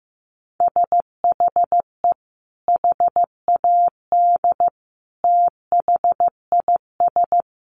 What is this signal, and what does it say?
Morse code, heard as a keyed tone, about 15 wpm, 700 Hz. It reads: SHE HAD THIS
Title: Sentences from Top 1200 Words 15wpm